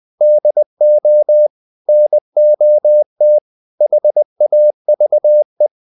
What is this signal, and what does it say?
Morse code, heard as a keyed tone, about 20 wpm, 600 Hz. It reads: DO NOT HAVE
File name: the old man and the sea part 2 20wpm